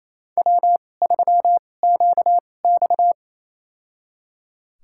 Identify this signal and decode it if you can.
Morse code, heard as a keyed tone, about 28 wpm, spaced as for 20 wpm, 700 Hz. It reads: W3QX
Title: USA Call Signs - 4 Characters - 2x Character Spacing - 28wpm